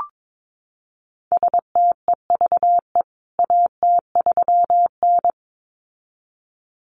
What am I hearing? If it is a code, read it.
Morse code, heard as a keyed tone, about 22 wpm, 700 Hz. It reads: STEVE AT3N
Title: CWT Exchange 22wpm